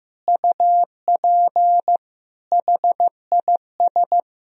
Morse code, heard as a keyed tone, about 16 wpm, 700 Hz. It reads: UP HIS
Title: Sentences from Top 1000 Words - Review Part 2 - 15wpm